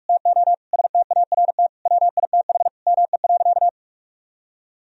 Morse code, heard as a keyed tone, about 46 wpm, 700 Hz. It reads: TO START WITH ME.
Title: Sentences from Top 1000 Words - Review Part 1 - 45wpm